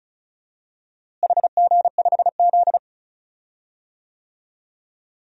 Morse code, heard as a keyed tone, about 36 wpm, 700 Hz. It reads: HG5Z
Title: DXCC-100 Call Signs - 4 Characters - Repeated 2x - 35wpm